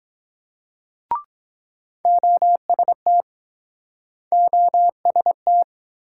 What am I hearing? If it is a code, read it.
Morse code, heard as a keyed tone, about 24 wpm, 700 Hz. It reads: OST OST